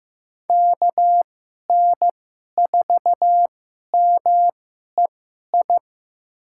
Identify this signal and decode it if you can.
Morse code, heard as a keyed tone, about 15 wpm, spaced as for 11 wpm, 700 Hz. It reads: KN4MEI
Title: USA Call Signs - 2x Character Spacing - 15wpm